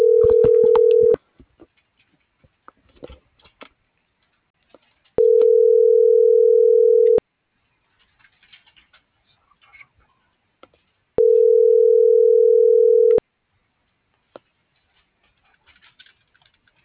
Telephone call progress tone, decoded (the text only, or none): ringback tone